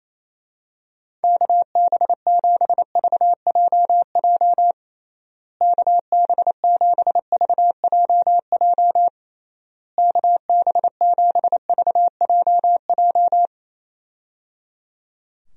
Morse code, heard as a keyed tone, about 28 wpm, 700 Hz. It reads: KB7VJJ KB7VJJ KB7VJJ
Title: USA Call Signs - 6 Characters - Repeated 3x - 28wpm